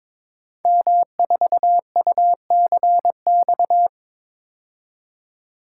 Morse code, heard as a keyed tone, about 22 wpm, 700 Hz. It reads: M4UCX
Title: Code Groups - 5 Letter:Numbers - 22wpm